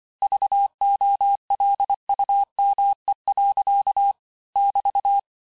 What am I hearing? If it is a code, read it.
Morse code, heard as a keyed tone, about 25 wpm, 800 Hz. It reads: VOLUME. <BT>
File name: cours-du-lundi files / cours-1583--texte-clair-18-puis-23-mpm